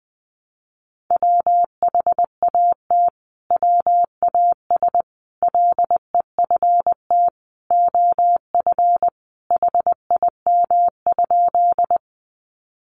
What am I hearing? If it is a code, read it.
Morse code, heard as a keyed tone, about 20 wpm, 700 Hz. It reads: WHAT WAS LEFT OF HIM?
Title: Sentences from Top 300 Words 20wpm